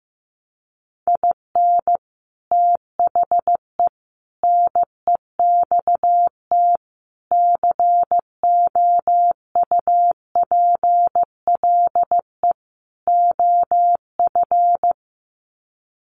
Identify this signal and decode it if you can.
Morse code, heard as a keyed tone, about 15 wpm, 700 Hz. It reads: IN THE NEXT COUPLE OF
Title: Sets of 5 Words Encore2 15wpm